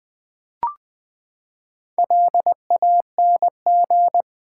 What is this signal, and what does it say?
Morse code, heard as a keyed tone, about 20 wpm, 700 Hz. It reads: LANG
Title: Build A Word - 4 Letters At A Time - 8-Letter Words - 20wpm